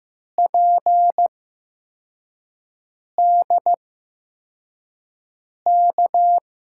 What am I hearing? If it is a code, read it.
Morse code, heard as a keyed tone, about 15 wpm, 700 Hz. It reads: P D K
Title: USA Call Signs - 5 Characters - 8x Character Spacing - 15wpm